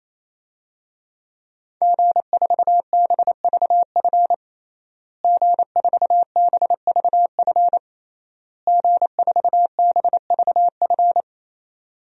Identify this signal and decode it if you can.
Morse code, heard as a keyed tone, about 27 wpm, 700 Hz. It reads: G4BVF G4BVF G4BVF
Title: DXCC-100 Call Signs - 5 Characters - Repeated 3x - 28wpm